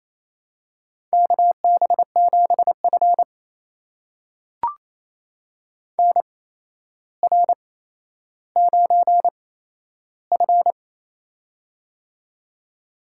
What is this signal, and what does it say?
Morse code, heard as a keyed tone, about 28 wpm, spaced as for 12 wpm, 700 Hz. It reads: KB7F NR9F
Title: USA Call Signs - 4 Characters - 8x Character Spacing - 28wpm